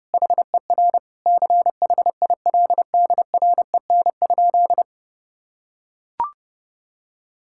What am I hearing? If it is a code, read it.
Morse code, heard as a keyed tone, about 30 wpm, 700 Hz. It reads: HER CHILDREN?
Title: Sentences from Top 1000 Words - Review Part 6 - 30wpm